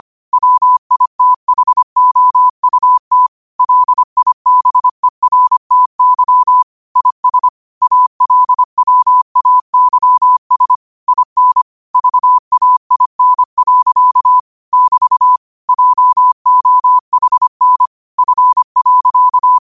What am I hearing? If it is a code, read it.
Morse code, heard as a keyed tone, about 25 wpm, 1000 Hz. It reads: WITHOUT LIBERTY IS ALWAYS IN VAIN. <BT> JOHN F.